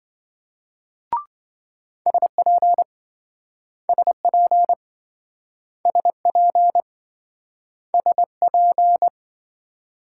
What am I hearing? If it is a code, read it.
Morse code, heard as a keyed tone, about 28 wpm, 700 Hz. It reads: SP SP SP SP